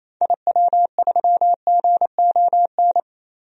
Morse code, heard as a keyed tone, about 28 wpm, 700 Hz. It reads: IW3GON
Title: DXCC-100 Call Signs - 6 Characters - Repeated 3x - 28wpm